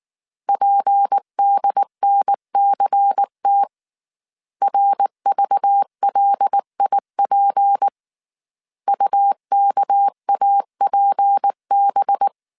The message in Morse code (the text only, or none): PDNCT RVLIP UKAPB